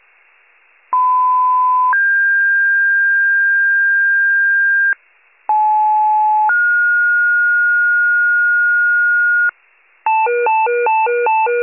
Two-tone page, heard as a one-second tone, then a three-second tone, about 1010 then 1600 Hz, about 860 then 1430 Hz, with noise. It is heard from 0.9 to 9.5 s.